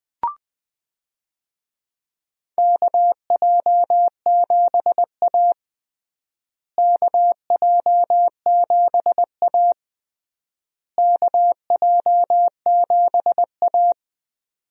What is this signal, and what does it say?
Morse code, heard as a keyed tone, about 20 wpm, 700 Hz. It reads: KJ7A KJ7A KJ7A
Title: USA Call Signs - 4 Characters - Repeated 3x - 20wpm